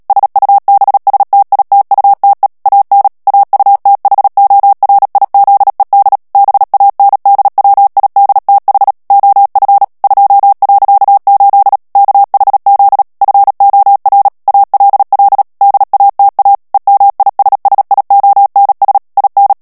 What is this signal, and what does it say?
Morse code, heard as a keyed tone, about 37 wpm, 800 Hz. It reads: SUBSTITUTE AN AUTHORIZED BANDWIDTH OF 2.8 KHZ FOR ALL DATA EMISSIONS IN
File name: morse mp3 / ANGLAIS texte clair 3 vitesse 35 mots minute